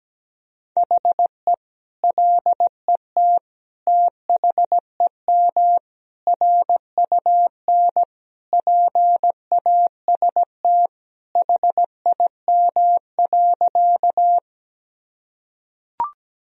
Morse code, heard as a keyed tone, about 17 wpm, 700 Hz. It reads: HE LET THEM RUN PAST HIM.
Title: Sentences from Top 1000 Words - Review Part 3 - 17wpm